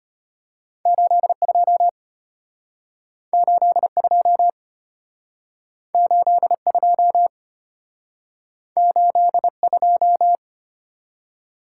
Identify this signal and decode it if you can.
Morse code, heard as a keyed tone, about 34 wpm, 700 Hz. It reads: 82 82 82 82